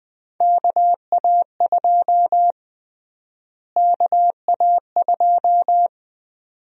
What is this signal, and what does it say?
Morse code, heard as a keyed tone, about 20 wpm, 700 Hz. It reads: KA2 KA2
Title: USA Call Sign Prefixes - Repeated 2x - 20wpm